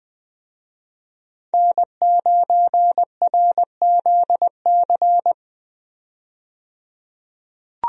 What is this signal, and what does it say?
Morse code, heard as a keyed tone, about 20 wpm, 700 Hz. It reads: N9RZC
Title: USA Call Signs - 5 Characters - 20wpm